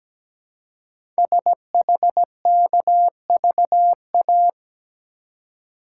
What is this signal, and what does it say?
Morse code, heard as a keyed tone, about 17 wpm, 700 Hz. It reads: SHKVA
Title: Code Groups - 5 Letters - 17wpm